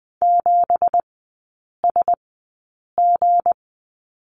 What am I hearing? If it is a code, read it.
Morse code, heard as a keyed tone, about 20 wpm, spaced as for 8 wpm, 700 Hz. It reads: 7SG
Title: 172 Lesson 28e - Number 9 - Three Characters 20wpm